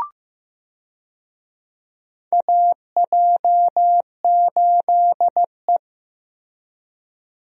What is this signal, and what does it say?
Morse code, heard as a keyed tone, about 15 wpm, 700 Hz. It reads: AJ8E
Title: USA Call Signs - 4 Characters - Repeated 2x - 15wpm